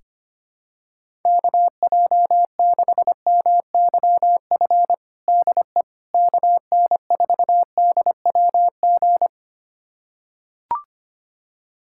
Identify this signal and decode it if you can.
Morse code, heard as a keyed tone, about 25 wpm, 700 Hz. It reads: KJ6MYF DE KN4DWG
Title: USA Callsign DE Callsign 25wpm